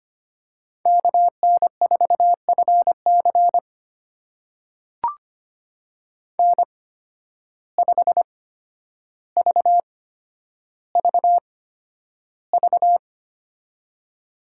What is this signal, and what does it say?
Morse code, heard as a keyed tone, about 25 wpm, spaced as for 11 wpm, 700 Hz. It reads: KN4FC N5VVV
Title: USA Call Signs - 5 Characters - 8x Character Spacing - 25wpm